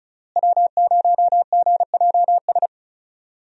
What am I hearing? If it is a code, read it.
Morse code, heard as a keyed tone, about 36 wpm, 700 Hz. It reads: W0GJS
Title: USA Call Signs - 5 Characters - Repeated 3x - 35wpm